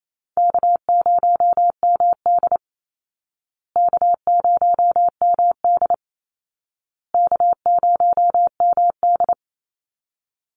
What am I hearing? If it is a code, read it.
Morse code, heard as a keyed tone, about 28 wpm, 700 Hz. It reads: K0MD K0MD K0MD
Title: Easy Call Signs Repeated 3x - 28wpm